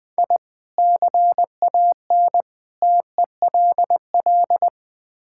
Morse code, heard as a keyed tone, about 20 wpm, 700 Hz. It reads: I CAN TELL
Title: Build A Phrase - 5-Word Phrases - 20wpm